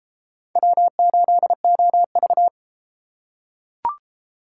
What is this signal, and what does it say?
Morse code, heard as a keyed tone, about 31 wpm, 700 Hz. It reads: W8OV